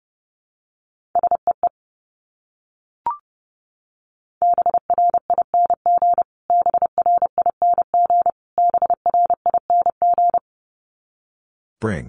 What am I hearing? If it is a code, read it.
Morse code, heard as a keyed tone, about 30 wpm, 700 Hz. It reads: SEE BRING BRING BRING
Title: Top 500 Words Repeated 30wpm